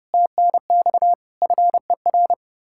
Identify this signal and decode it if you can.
Morse code, heard as a keyed tone, about 40 wpm, 700 Hz. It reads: TNX FER